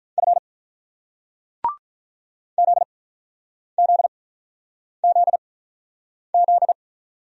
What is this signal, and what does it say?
Morse code, heard as a keyed tone, about 46 wpm, spaced as for 9 wpm, 700 Hz. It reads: F ZZZZ